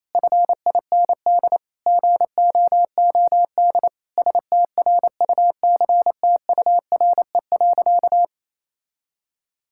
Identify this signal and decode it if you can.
Morse code, heard as a keyed tone, about 27 wpm, 700 Hz. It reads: FIND GOOD STRUCTURE.
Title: Sentences from Top 1000 Words - Review Part 6 - 28wpm